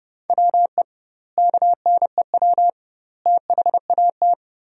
Morse code, heard as a keyed tone, about 30 wpm, 700 Hz. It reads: WE KNEW THAT
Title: 2x Word-Spacing - Sentences from Top 400 Words - 30wpm